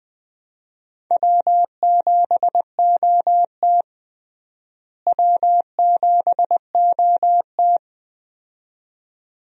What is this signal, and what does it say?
Morse code, heard as a keyed tone, about 20 wpm, 700 Hz. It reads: W7OT W7OT
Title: USA Call Signs - 4 Characters - Repeated 2x - 20wpm